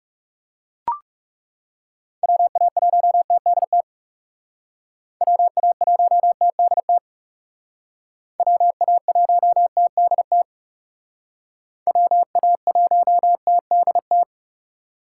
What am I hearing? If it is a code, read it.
Morse code, heard as a keyed tone, about 38 wpm, 700 Hz. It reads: WA1TDT WA1TDT WA1TDT WA1TDT
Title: Speed Racing - USA 2x3 Call Signs 30wpm